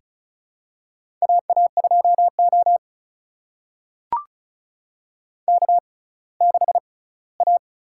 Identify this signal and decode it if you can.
Morse code, heard as a keyed tone, about 36 wpm, spaced as for 20 wpm, 700 Hz. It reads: AA2O K6A
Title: USA Call Signs - 4 Characters - 6x Character Spacing - 35wpm